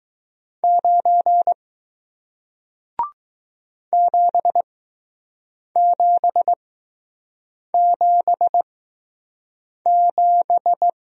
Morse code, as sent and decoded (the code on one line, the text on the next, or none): ----. / --... --... --... --...
9 7777